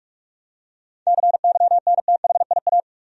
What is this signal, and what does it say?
Morse code, heard as a keyed tone, about 45 wpm, 700 Hz. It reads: CYNTHIA
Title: USA Common Names - 45wpm